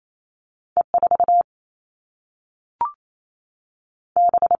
Morse code, heard as a keyed tone, about 28 wpm, 700 Hz. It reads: E4 B
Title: All Cut Numbers - 2-Digit 28wpm